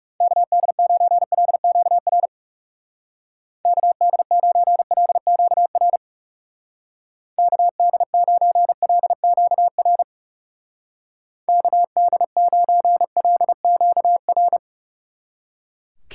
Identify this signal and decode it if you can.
Morse code, heard as a keyed tone, about 38 wpm, 700 Hz. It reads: KD9LQR KD9LQR KD9LQR KD9LQR